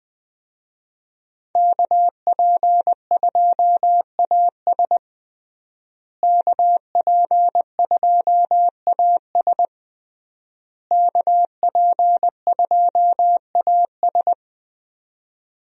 Morse code, heard as a keyed tone, about 20 wpm, 700 Hz. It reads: KP2AS KP2AS KP2AS
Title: USA Call Signs - 5 Characters - Repeated 3x - 20wpm